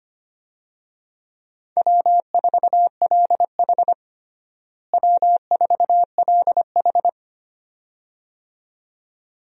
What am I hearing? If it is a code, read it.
Morse code, heard as a keyed tone, about 25 wpm, 700 Hz. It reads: W4LH W4LH
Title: USA Call Signs - 4 Characters - Repeated 2x - 25wpm